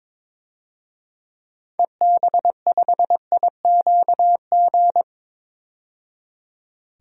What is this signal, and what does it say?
Morse code, heard as a keyed tone, about 22 wpm, 700 Hz. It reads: EB5IQG
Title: DXCC-100 Call Signs - 6 Characters - 22wpm